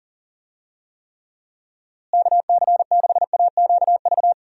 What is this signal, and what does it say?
Morse code, heard as a keyed tone, about 39 wpm, 700 Hz. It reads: KC6AQV
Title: USA Call Signs - 6 Characters - Repeated 2x - 40wpm